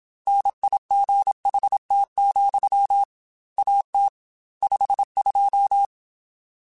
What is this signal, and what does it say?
Morse code, heard as a keyed tone, about 27 wpm, 800 Hz. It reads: NIGHT, AT 52
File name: Weather Sets / Weather Sets-mp3s / Weather 4 17 @25